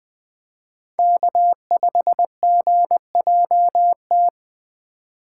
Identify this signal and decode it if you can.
Morse code, heard as a keyed tone, about 23 wpm, 700 Hz. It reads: K5GJT
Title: Speed Racing - USA Call Signs 20wpm